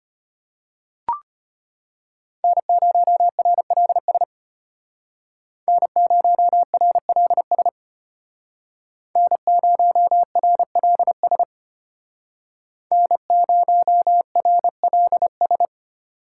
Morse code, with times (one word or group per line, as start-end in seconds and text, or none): 2.4-4.2 N0RLS
5.7-7.7 N0RLS
9.2-11.4 N0RLS
12.9-15.7 N0RLS